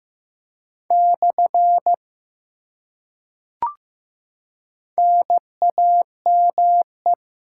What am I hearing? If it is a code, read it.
Morse code, heard as a keyed tone, about 15 wpm, 700 Hz. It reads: / NAME
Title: Top 90 QSO Elements 15wpm